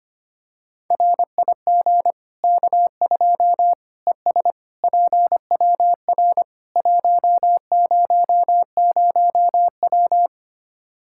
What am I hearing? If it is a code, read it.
Morse code, heard as a keyed tone, about 25 wpm, 700 Hz. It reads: RIG K2 ES PWR 100W
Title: Practice QSO - Exchange 2 - Info - 25wpm